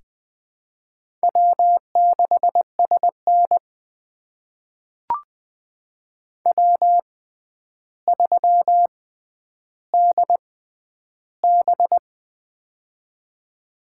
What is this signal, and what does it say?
Morse code, heard as a keyed tone, about 20 wpm, spaced as for 10 wpm, 700 Hz. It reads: W6SN W3DB
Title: USA Call Signs - 4 Characters - 6x Character Spacing - 20wpm